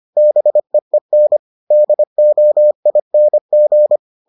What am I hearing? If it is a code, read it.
Morse code, heard as a keyed tone, about 25 wpm, 600 Hz. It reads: BEEN DOING